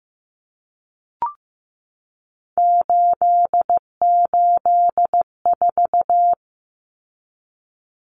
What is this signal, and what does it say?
Morse code, heard as a keyed tone, about 15 wpm, 700 Hz. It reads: 884
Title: Common Cut Numbers - 3-Digit 15wpm